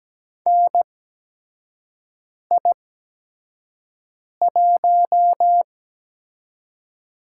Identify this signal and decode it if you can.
Morse code, heard as a keyed tone, about 17 wpm, spaced as for 5 wpm, 700 Hz. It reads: NI1
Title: USA Call Signs - 4 Characters - 8x Character Spacing - 17wpm